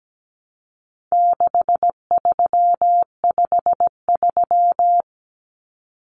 Morse code, heard as a keyed tone, about 17 wpm, 700 Hz. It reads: 6353
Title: Numbers - 4-Digit 17wpm